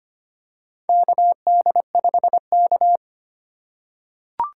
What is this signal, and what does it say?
Morse code, heard as a keyed tone, about 25 wpm, 700 Hz. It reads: KD5K
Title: USA Call Signs - 4 Characters - 2x Character Spacing - 25wpm